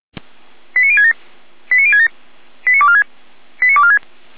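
Five-tone selective call (ZVEI), heard as quick runs of five tones, tones of about 80 ms. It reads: 89067 89067 89246 89246
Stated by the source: ZVEI five-tone selective call received by SDR, logged by feuerwehr-orlamuende (fire brigade Orlamünde).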